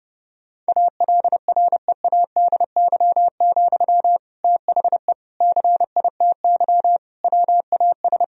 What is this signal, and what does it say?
Morse code, heard as a keyed tone, about 30 wpm, 700 Hz. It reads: ALREADY, THE CITY WAS
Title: Sentences from Top 1000 Words - Review Part 4 - 30wpm